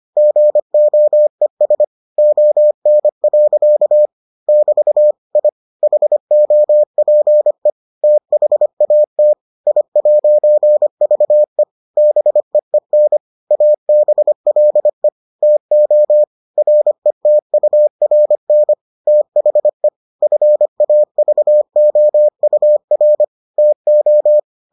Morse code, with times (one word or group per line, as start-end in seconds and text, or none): 0.2-1.8 GOES
2.2-4.1 ON.
4.5-5.1 <BT>
5.4-5.5 I
5.8-7.7 HOPE
8.0-9.3 THAT
9.7-11.6 I'VE
12.0-13.2 BEEN
13.5-15.1 ABLE
15.4-16.2 TO
16.6-18.7 RETURN
19.1-19.9 THE
20.2-23.2 FAVOUR
23.6-24.4 TO